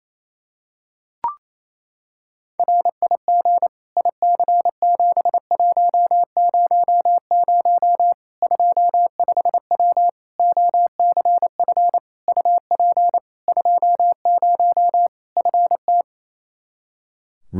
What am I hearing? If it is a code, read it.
Morse code, heard as a keyed tone, about 28 wpm, 700 Hz. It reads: RIG IC7100 25W OCF UP 20 FT
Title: K1USN SST QSO Practice 28wpm